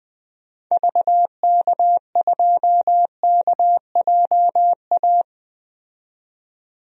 Morse code, heard as a keyed tone, about 20 wpm, 700 Hz. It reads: VK2KJA
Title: DXCC-100 Call Signs - 6 Characters - 20wpm